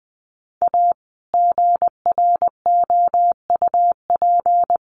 Morse code, heard as a keyed tone, about 20 wpm, 700 Hz. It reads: A GROUP
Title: Sets of 3 Words 20wpm